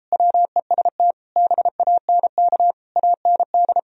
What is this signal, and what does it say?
Morse code, heard as a keyed tone, about 33 wpm, 700 Hz. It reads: WEST BANK AND